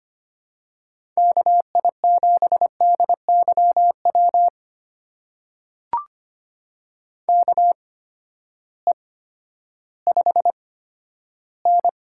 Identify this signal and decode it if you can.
Morse code, heard as a keyed tone, about 25 wpm, spaced as for 13 wpm, 700 Hz. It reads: KI7DYW KE5N
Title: USA Call Signs - 6 Characters - 8x Character Spacing - 25wpm